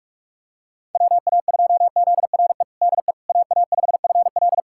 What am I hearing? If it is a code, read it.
Morse code, heard as a keyed tone, about 42 wpm, 690 Hz. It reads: WA2ZRE DE AA5FL
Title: USA Callsign DE Callsign 45wpm